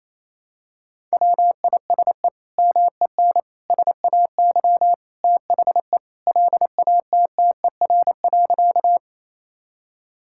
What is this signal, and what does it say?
Morse code, heard as a keyed tone, about 28 wpm, 700 Hz. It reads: WISE MEN SAY THE LATTER.
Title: Sentences from Top 1600 Words 28wpm